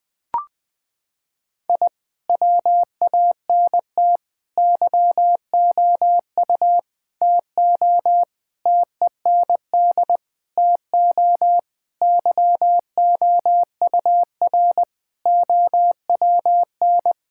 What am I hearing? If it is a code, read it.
Morse code, heard as a keyed tone, about 21 wpm, 700 Hz. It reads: I WANT YOU TO TEND TO YOUR OWN
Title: Sentences from Top 1000 Words - Review Part 5 - 20wpm